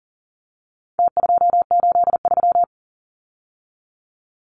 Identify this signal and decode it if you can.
Morse code, heard as a keyed tone, about 40 wpm, 700 Hz. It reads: T283
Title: Numbers - Contest Style - 4-Digit 40wpm